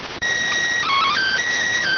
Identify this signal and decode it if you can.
SSTV (slow-scan television), PD 290.